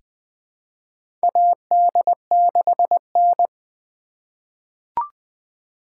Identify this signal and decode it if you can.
Morse code, heard as a keyed tone, about 21 wpm, 700 Hz. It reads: AD6N
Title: USA Call Signs - 4 Characters - 4x Character Spacing - 20wpm